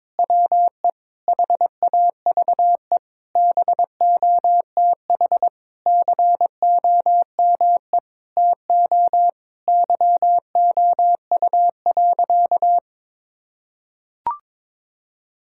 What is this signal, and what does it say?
Morse code, heard as a keyed tone, about 22 wpm, 700 Hz. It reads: WE HAVE BOTH COME TO YOU.
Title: Sentences from Top 1600 Words 22wpm